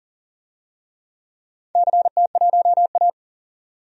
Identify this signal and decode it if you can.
Morse code, heard as a keyed tone, about 40 wpm, 700 Hz. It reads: CT1A
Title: DXCC-100 Call Signs - 4 Characters - 40wpm